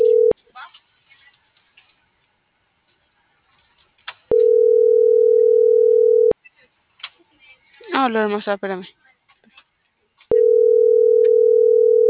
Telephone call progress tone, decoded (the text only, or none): ringback tone